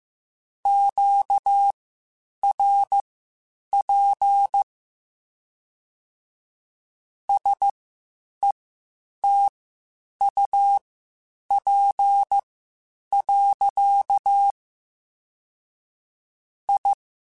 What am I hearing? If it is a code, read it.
Morse code, heard as a keyed tone, about 15 wpm, spaced as for 8 wpm, 800 Hz. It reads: QRP SETUP. I
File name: (spaced) / QSOs Spaced-mp3s / WD0BGP 1 @13 sp